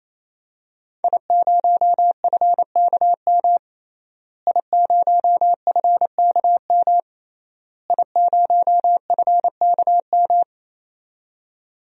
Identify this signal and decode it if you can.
Morse code, heard as a keyed tone, about 28 wpm, 700 Hz. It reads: I0FKM I0FKM I0FKM
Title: DXCC-100 Call Signs - 5 Characters - Repeated 3x - 28wpm